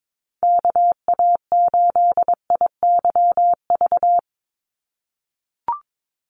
Morse code, heard as a keyed tone, about 22 wpm, 700 Hz. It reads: KA8IYV
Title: USA Call Signs 22wpm